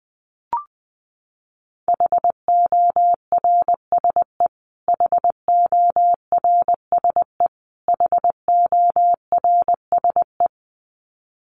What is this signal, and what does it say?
Morse code, heard as a keyed tone, about 20 wpm, 700 Hz. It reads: HORSE HORSE HORSE
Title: Top 300 Words Repeated 20wpm